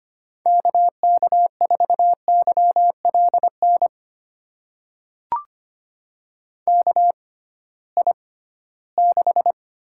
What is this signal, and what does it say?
Morse code, heard as a keyed tone, about 25 wpm, spaced as for 16 wpm, 700 Hz. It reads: KK4YLN KI6
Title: USA Call Signs - 6 Characters - 6x Character Spacing - 25wpm